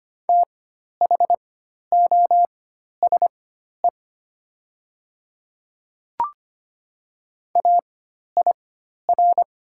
Morse code, heard as a keyed tone, about 25 wpm, spaced as for 12 wpm, 700 Hz. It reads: THOSE AIR
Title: 4x Character Spacing - Top 500 Words - 25wpm